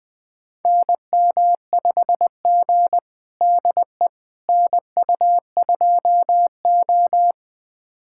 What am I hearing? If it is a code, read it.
Morse code, heard as a keyed tone, about 20 wpm, 700 Hz. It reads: NM5G DE NU2O
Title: USA Callsign DE Callsign - 4 Characters - 20wpm